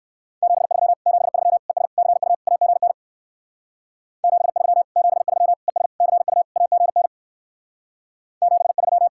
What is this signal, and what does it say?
Morse code, heard as a keyed tone, about 57 wpm, 700 Hz. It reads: 73 73 ES CU AGN 73 73 ES CU AGN 73